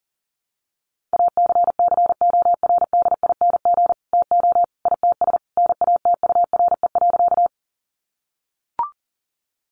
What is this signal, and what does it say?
Morse code, heard as a keyed tone, about 40 wpm, 700 Hz. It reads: ACCORDING TO ITS NATURE.
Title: Sentences from Top 700 Words 40wpm